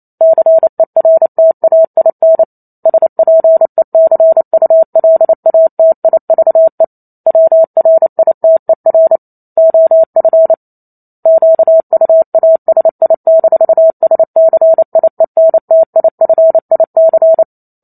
Morse code, heard as a keyed tone, about 28 wpm, 650 Hz. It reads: CERTAIN SPECULATIVE WRITER OF QUASI-SCIENTIFIC